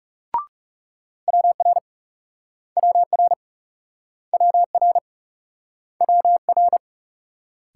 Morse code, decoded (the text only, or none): WR WR WR WR